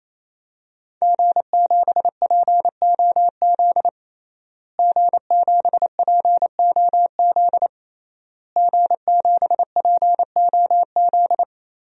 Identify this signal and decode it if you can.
Morse code, heard as a keyed tone, about 27 wpm, 700 Hz. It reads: G7POZ G7POZ G7POZ
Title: DXCC-100 Call Signs - 5 Characters - Repeated 3x - 28wpm